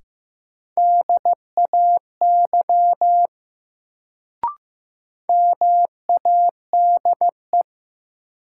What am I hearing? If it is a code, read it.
Morse code, heard as a keyed tone, about 15 wpm, 700 Hz. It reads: DAY MADE
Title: Sentences from Top 200 Words - New Word Drill - 15wpm